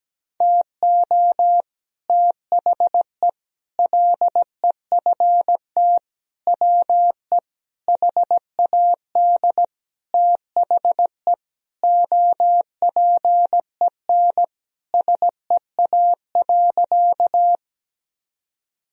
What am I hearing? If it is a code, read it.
Morse code, heard as a keyed tone, about 17 wpm, 700 Hz. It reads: TO THE LEFT WE HAD THE OPEN SEA.
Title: Sentences from Top 1000 Words - Review Part 1 - 17wpm